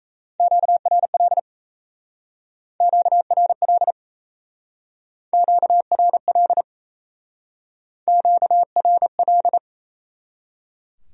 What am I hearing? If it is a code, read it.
Morse code, heard as a keyed tone, about 36 wpm, 700 Hz. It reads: QRL QRL QRL QRL